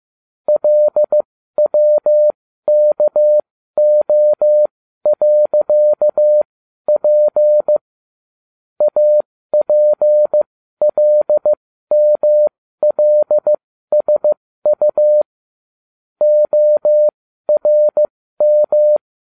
Morse code, heard as a keyed tone, about 15 wpm, spaced as for 12 wpm, 600 Hz. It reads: LWKO.P APLMLSU ORM
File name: cw / ubacw12 2